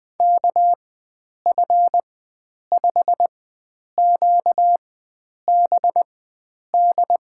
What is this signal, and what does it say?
Morse code, heard as a keyed tone, about 20 wpm, spaced as for 9 wpm, 700 Hz. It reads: KF5QBD
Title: USA Call Signs - 6 Characters - 4x Character Spacing - 20wpm